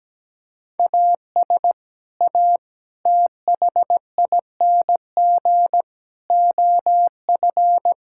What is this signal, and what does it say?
Morse code, heard as a keyed tone, about 17 wpm, 700 Hz. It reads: AS A THING OF
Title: Sentences from Top 1000 Words - Review Part 6 - 17wpm